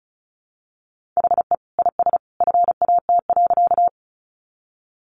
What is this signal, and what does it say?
Morse code, heard as a keyed tone, about 34 wpm, 700 Hz. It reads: HE IS FAT.
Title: Easy Phrases 35wpm